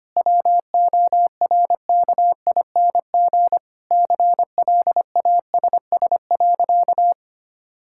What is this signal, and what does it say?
Morse code, heard as a keyed tone, about 25 wpm, 700 Hz. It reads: WORKING CLASS.
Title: Sentences from Top 1000 Words - Review Part 6 - 25wpm